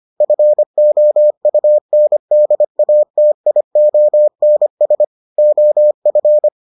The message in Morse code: FOUNDATIONS OF

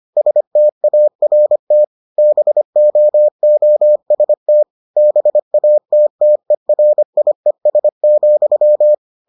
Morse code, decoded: START BOOST BATTERIES,

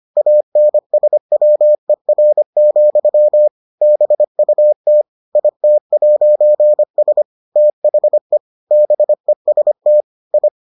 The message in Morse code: ANSWER, BUT IT'S THE BEST I

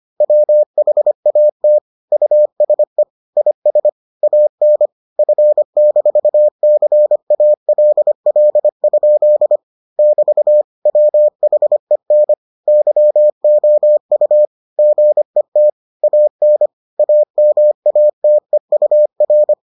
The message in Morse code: WHAT USE IS AN F-CALL? <BT> WHEN YOU GET AN AMATEUR

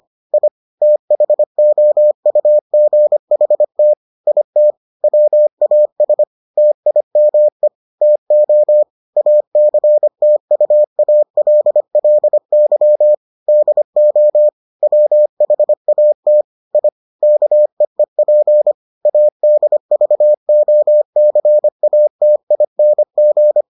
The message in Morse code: I THOUGHT IT WAS TIME TO ACTUALLY DO WHAT I KEEP ADVOCATING

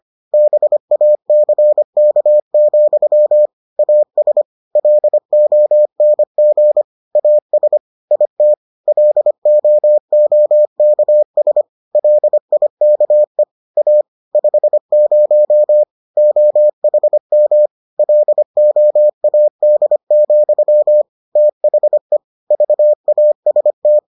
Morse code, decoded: BACK, AS LONG AS IT LOOKS LIKE A 50 OHM LOAD, THE VAST

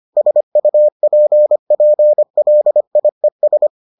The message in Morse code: SUPPLIES

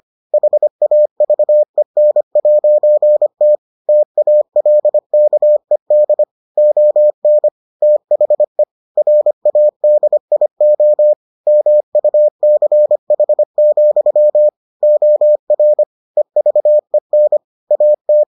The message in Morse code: HAVEN'T TALKED ON THE RADIO MUCH, OR EVEN AT

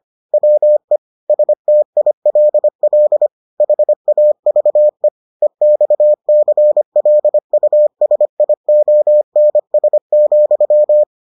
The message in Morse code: WE STILL HAVE EXCLUSIONS,

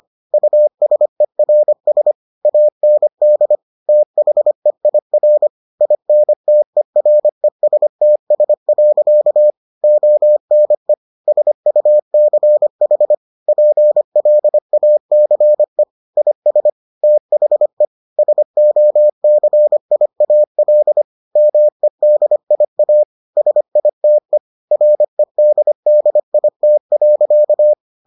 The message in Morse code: USERS AND THEIR INTERESTS. ONE SUCH PLACE IS THE SOCIAL MEDIA SITE REDDIT.

